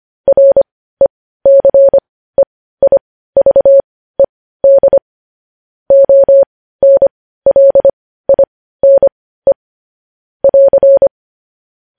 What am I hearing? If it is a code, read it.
Morse code, heard as a keyed tone, about 25 wpm, spaced as for 15 wpm, 550 Hz. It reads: RECEIVED ONLINE <AR>
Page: News Headlines in Morse Code at 15 WPM